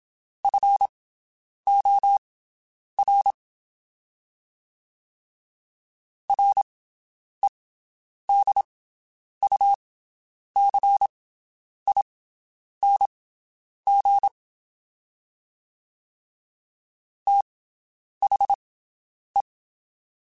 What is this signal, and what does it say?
Morse code, heard as a keyed tone, about 27 wpm, spaced as for 9 wpm, 800 Hz. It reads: FOR REDUCING THE
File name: QSOs Spaced